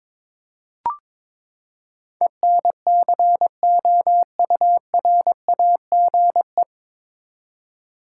Morse code, heard as a keyed tone, about 22 wpm, 700 Hz. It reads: ENCOURAGE
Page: Morse Code Ninja practice 22wpm